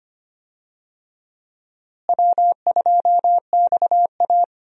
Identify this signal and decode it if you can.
Morse code, heard as a keyed tone, about 25 wpm, 700 Hz. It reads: W2XA
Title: USA Call Signs - 4 Characters - Repeated 2x - 25wpm